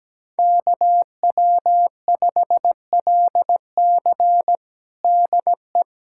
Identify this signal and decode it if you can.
Morse code, heard as a keyed tone, about 17 wpm, 700 Hz. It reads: KW5LC DE
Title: USA Callsign DE Callsign - 5 Characters - 17wpm